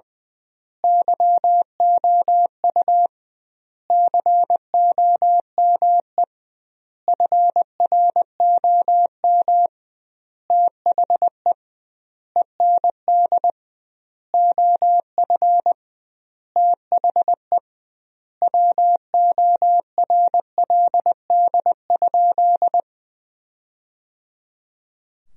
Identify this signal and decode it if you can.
Morse code, heard as a keyed tone, about 20 wpm, 700 Hz. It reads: YOU COME FROM THE END OF THE WORLD?
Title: 2x Word-Spacing - Sentences from Top 200 Words - 20wpm